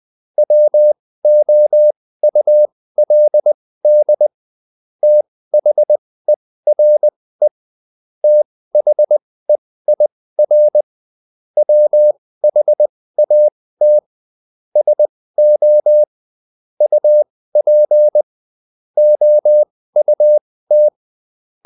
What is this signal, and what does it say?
Morse code, heard as a keyed tone, about 20 wpm, spaced as for 15 wpm, 600 Hz. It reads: WOULD THERE THEIR WHAT SO UP OUT
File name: Top-100-Words-15